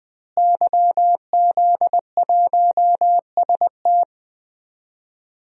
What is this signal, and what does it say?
Morse code, heard as a keyed tone, about 20 wpm, 700 Hz. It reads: YZ1ST
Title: DXCC-100 Call Signs - 5 Characters - Repeated 3x - 20wpm